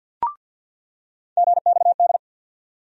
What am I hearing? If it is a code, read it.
Morse code, heard as a keyed tone, about 49 wpm, 690 Hz. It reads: GXD
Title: USA Call Sign Postfixes - 3 Letters 50wpm